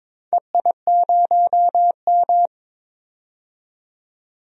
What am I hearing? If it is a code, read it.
Morse code, heard as a keyed tone, about 22 wpm, 700 Hz. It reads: EI0M
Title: DXCC-100 Call Signs - 4 Characters - 22wpm